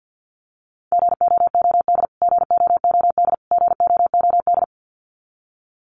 Morse code, heard as a keyed tone, about 49 wpm, 700 Hz. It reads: GOOD GOOD GOOD
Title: Top 500 Words Repeated 50wpm